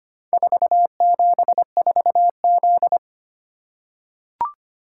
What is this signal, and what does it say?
Morse code, heard as a keyed tone, about 25 wpm, 700 Hz. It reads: 474Z
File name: practice / 25wpm / Four Letter-Number Combinations - 25wpm